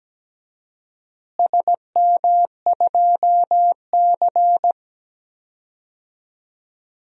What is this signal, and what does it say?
Morse code, heard as a keyed tone, about 17 wpm, 700 Hz. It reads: SM2C
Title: DXCC-100 Call Signs - 4 Characters - Repeated 2x - 17wpm